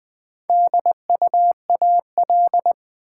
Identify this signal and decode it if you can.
Morse code, heard as a keyed tone, about 20 wpm, 700 Hz. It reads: DUAL
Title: Build A Word - 1 Letter At A Time - 3 to 5-Letter Words - 20wpm